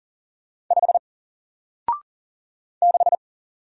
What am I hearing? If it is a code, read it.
Morse code, heard as a keyed tone, about 40 wpm, 700 Hz. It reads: 5 6
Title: Single Letter-Number Alt 40wpm